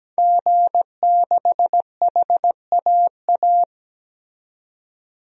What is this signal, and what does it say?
Morse code, heard as a keyed tone, about 17 wpm, 700 Hz. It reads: G6HAA
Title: DXCC-100 Call Signs - 5 Characters - 17wpm